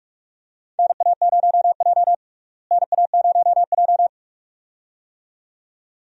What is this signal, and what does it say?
Morse code, heard as a keyed tone, about 46 wpm, 700 Hz. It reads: NA0J NA0J
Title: USA Call Signs - 4 Characters - Repeated 2x - 45wpm